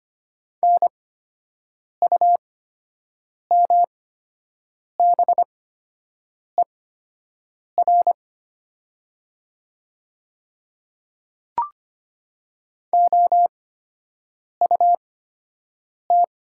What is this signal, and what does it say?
Morse code, heard as a keyed tone, about 25 wpm, spaced as for 7 wpm, 700 Hz. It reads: NUMBER OUT
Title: 8x Character Spacing - Top 100 Words - 25wpm